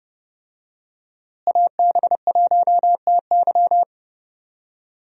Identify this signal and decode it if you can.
Morse code, heard as a keyed tone, about 32 wpm, 700 Hz. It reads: AB1TY
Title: USA Call Signs - 5 Characters - Repeated 2x - 30wpm